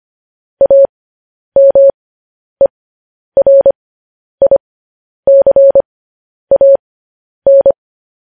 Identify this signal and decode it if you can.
Morse code, heard as a keyed tone, about 25 wpm, spaced as for 10 wpm, 550 Hz. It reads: AMERICAN
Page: news-headlines-in-morse-code-at-10-wpm